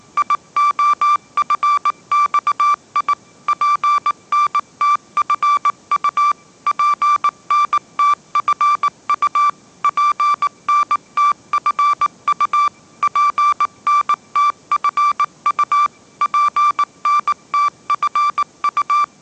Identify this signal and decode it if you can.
Morse code, heard as a keyed tone, about 19 wpm, 1170 Hz. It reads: IOFXI PNTFU PNTFU PNTFU PNTFU PNTFU
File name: 20wpm